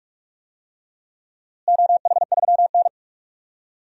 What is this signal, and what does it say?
Morse code, heard as a keyed tone, about 43 wpm, 700 Hz. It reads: OH3N
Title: DXCC-100 Call Signs - 4 Characters - Repeated 3x - 45wpm